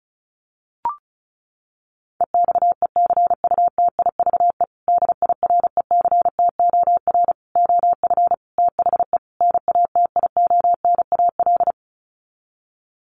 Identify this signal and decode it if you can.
Morse code, heard as a keyed tone, about 34 wpm, 700 Hz. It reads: EXECUTIVE DIRECTOR OF THE NATIONAL
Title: Sets of 5 Words Encore2 35wpm